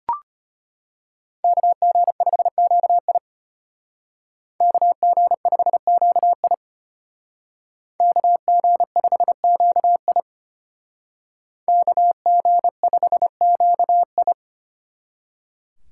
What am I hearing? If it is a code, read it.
Morse code, heard as a keyed tone, about 33 wpm, 700 Hz. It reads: KG5QI KG5QI KG5QI KG5QI